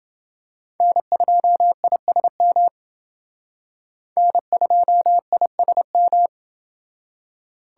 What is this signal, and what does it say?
Morse code, heard as a keyed tone, about 28 wpm, 700 Hz. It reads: N2ISM N2ISM